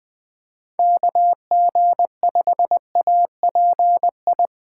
Morse code, heard as a keyed tone, about 20 wpm, 700 Hz. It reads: KG5API
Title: Speed Racing - USA 2x3 Call Signs 17wpm